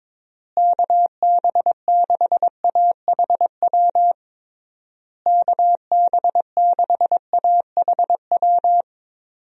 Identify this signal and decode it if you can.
Morse code, heard as a keyed tone, about 22 wpm, 700 Hz. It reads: KB6AHW KB6AHW
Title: USA Call Signs - 6 Characters - Repeated 3x - 22wpm